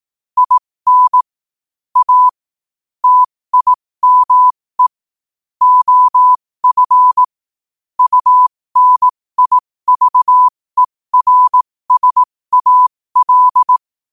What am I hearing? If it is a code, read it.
Morse code, heard as a keyed tone, about 18 wpm, spaced as for 15 wpm, 1000 Hz. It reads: IN A TIME OF UNIVERSAL
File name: qotdfiles / 15wpm